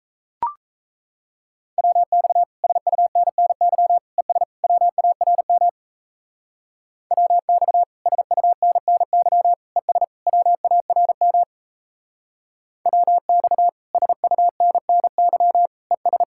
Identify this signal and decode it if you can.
Morse code, heard as a keyed tone, about 34 wpm, 700 Hz. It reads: WX SUNNY ES WARM WX SUNNY ES WARM WX SUNNY ES
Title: Speed-Racing Common QSO Phrases - 28wpm